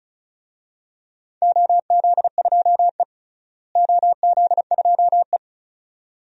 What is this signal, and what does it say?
Morse code, heard as a keyed tone, about 36 wpm, 700 Hz. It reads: OZ2E OZ2E
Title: DXCC-100 Call Signs - 4 Characters - Repeated 2x - 35wpm